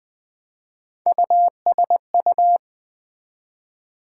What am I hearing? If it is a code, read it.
Morse code, heard as a keyed tone, about 20 wpm, 700 Hz. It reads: USU
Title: Build A Word - 1 Letter At A Time - 5-Letter Words - 20wpm